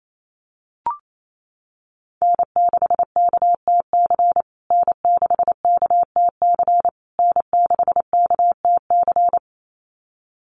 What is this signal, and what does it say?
Morse code, heard as a keyed tone, about 28 wpm, 700 Hz. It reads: N6KTC N6KTC N6KTC
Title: USA Call Signs Repeated 3x 28wpm